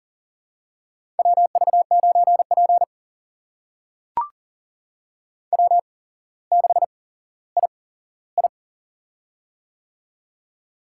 Morse code, heard as a keyed tone, about 40 wpm, spaced as for 17 wpm, 700 Hz. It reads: WV9P W6II
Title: USA Call Signs - 4 Characters - 8x Character Spacing - 40wpm